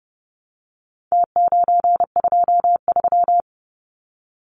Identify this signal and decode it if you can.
Morse code, heard as a keyed tone, about 30 wpm, 700 Hz. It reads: T923
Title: Numbers - Contest Style - 4-Digit 30wpm